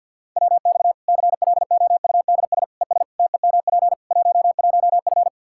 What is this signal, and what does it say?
Morse code, heard as a keyed tone, about 51 wpm, 700 Hz. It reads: WX CLOUDS ES TEMP 11F
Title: Practice QSO - Exchange 2 - Info - 50wpm